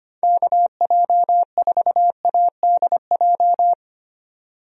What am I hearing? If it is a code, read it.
Morse code, heard as a keyed tone, about 25 wpm, 700 Hz. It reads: KJ4ADJ
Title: USA Call Signs - 6 Characters - 25wpm